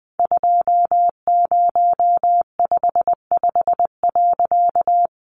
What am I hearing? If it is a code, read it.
Morse code, heard as a keyed tone, about 20 wpm, 700 Hz. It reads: 2055.
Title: A Sound Of Thunder 20wpm